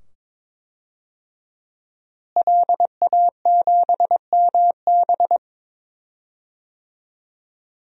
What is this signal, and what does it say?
Morse code, heard as a keyed tone, about 22 wpm, 700 Hz. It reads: LA7MB
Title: DXCC-100 Call Signs - 5 Characters - Repeated 2x - 22wpm